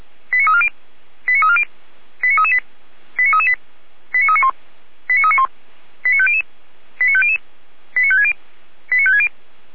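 Five-tone selective call (ZVEI), heard as quick runs of five tones, tones of about 70 ms. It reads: 89340 89340 89338 89338 89391 89391 89500 89500 89560 89560
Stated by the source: ZVEI five-tone selective call received by SDR.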